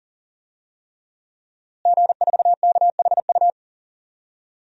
Morse code, heard as a keyed tone, about 40 wpm, 700 Hz. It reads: G4KHU
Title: DXCC-100 Call Signs - 5 Characters - Repeated 3x - 40wpm